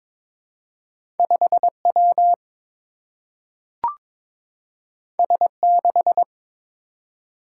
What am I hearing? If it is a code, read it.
Morse code, heard as a keyed tone, about 23 wpm, 700 Hz. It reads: 5 W S 6
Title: Single-Letter Single-Number 22wpm